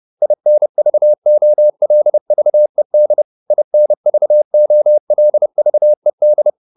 Morse code, heard as a keyed tone, about 30 wpm, 600 Hz. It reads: INVOLVED INVOLVED